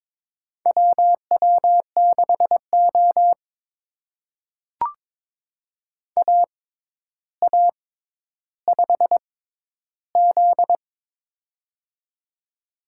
Morse code, heard as a keyed tone, about 22 wpm, spaced as for 11 wpm, 700 Hz. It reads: WW6O AA5Z
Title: USA Call Signs - 4 Characters - 6x Character Spacing - 22wpm